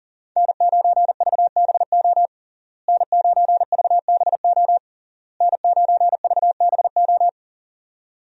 Morse code, heard as a keyed tone, about 40 wpm, 700 Hz. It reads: N9VBO N9VBO N9VBO
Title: USA Call Signs - 5 Characters - Repeated 3x - 40wpm